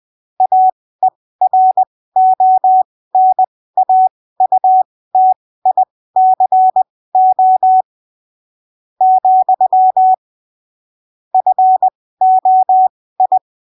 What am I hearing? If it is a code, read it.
Morse code, heard as a keyed tone, about 20 wpm, spaced as for 15 wpm, 750 Hz. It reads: AERONAUTICO , FOI
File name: arquivos / texto / texto 07 20ppm